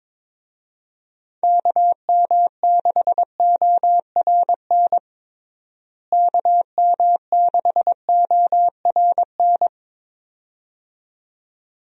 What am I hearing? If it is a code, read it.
Morse code, heard as a keyed tone, about 22 wpm, 700 Hz. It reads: KM6ORN KM6ORN
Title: USA Call Signs - 6 Characters - Repeated 2x - 22wpm